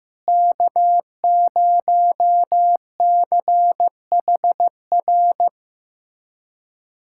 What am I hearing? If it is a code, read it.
Morse code, heard as a keyed tone, about 15 wpm, 700 Hz. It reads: K0CHR
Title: USA Callsign DE Callsign - 5 Characters - 15wpm